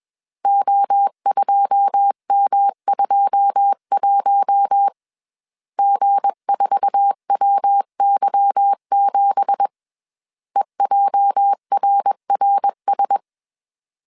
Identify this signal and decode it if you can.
Morse code, heard as a keyed tone, about 21 wpm, 800 Hz. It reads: O2M21 G4WY7 EJRRS
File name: CW / DET / 1 9 1500C S C 20wpm 2310